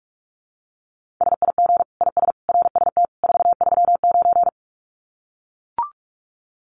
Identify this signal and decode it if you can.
Morse code, heard as a keyed tone, about 44 wpm, 700 Hz. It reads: SIG IS RST 439
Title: Common QSO Phrases 45wpm